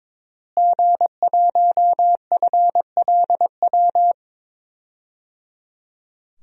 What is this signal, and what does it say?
Morse code, heard as a keyed tone, about 22 wpm, 700 Hz. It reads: G1FLW
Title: DXCC-100 Call Signs - 5 Characters - Repeated 3x - 22wpm